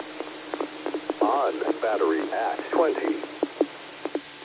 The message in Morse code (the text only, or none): none